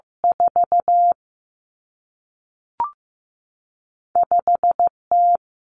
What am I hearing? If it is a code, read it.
Morse code, heard as a keyed tone, about 15 wpm, 700 Hz. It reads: 4 5 T
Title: Common Cut Numbers - 2-Digit 15wpm